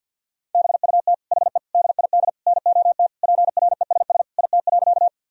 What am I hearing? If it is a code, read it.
Morse code, heard as a keyed tone, about 51 wpm, 700 Hz. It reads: BUT HE DID NOT PRESS IT.
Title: Sentences from Top 1000 Words - Review Part 1 - 50wpm